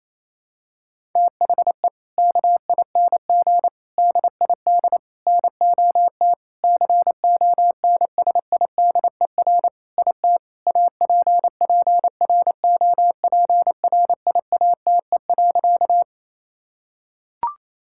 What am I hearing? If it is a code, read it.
Morse code, heard as a keyed tone, about 28 wpm, 700 Hz. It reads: THE KING DID NOT CONSIDER IT APPROPRIATE.
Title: Sentences from Top 1000 Words - Review Part 5 - 28wpm